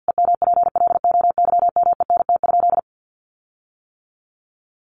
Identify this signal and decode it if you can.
Morse code, heard as a keyed tone, about 50 wpm, 700 Hz. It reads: EMPLOYMENT?
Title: Sentences from Top 1500 Words 50wpm